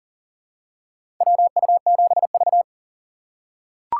Morse code, heard as a keyed tone, about 40 wpm, 700 Hz. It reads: WU7V
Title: USA Call Signs - 6x Character Spacing - 40wpm